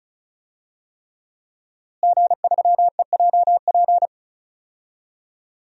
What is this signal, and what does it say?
Morse code, heard as a keyed tone, about 34 wpm, 700 Hz. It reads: G3EJP
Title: DXCC-100 Call Signs - 5 Characters - Repeated 3x - 35wpm